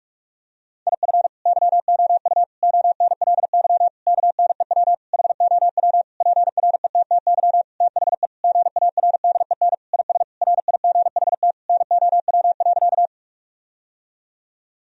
Morse code, heard as a keyed tone, about 41 wpm, 700 Hz. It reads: IF YOU ONLY KNEW HOW PRETTY THE GARDEN IS RIGHT NOW.
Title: Sentences from Top 1000 Words - Review Part 4 - 45wpm